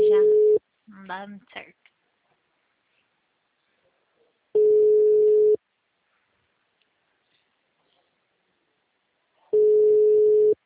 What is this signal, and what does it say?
Ringback tone (Europe, 425 Hz), with speech.